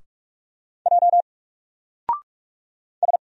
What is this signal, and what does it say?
Morse code, heard as a keyed tone, about 46 wpm, 700 Hz. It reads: J S
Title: Single Letter-Number 45wpm